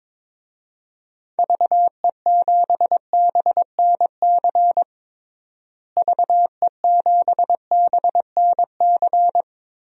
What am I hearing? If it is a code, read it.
Morse code, heard as a keyed tone, about 22 wpm, 700 Hz. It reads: VE7BNC VE7BNC
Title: DXCC-100 Call Signs - 6 Characters - Repeated 2x - 22wpm